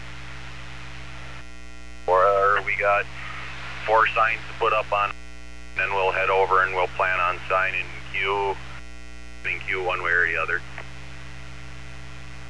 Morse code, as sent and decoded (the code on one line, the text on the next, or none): none